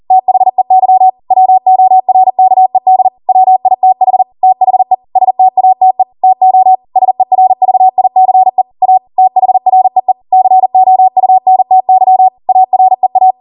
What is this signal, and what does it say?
Morse code, heard as a keyed tone, about 40 wpm, 750 Hz. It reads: THEY WORKED WITH THE STATE TO SERVICE A THREE COUNTY AREA